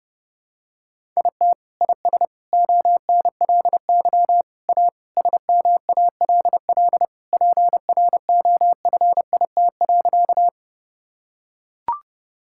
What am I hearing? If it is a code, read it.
Morse code, heard as a keyed tone, about 30 wpm, 700 Hz. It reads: IT IS ONLY A SMALL PROFIT.
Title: Sentences from Top 1000 Words - Review Part 5 - 30wpm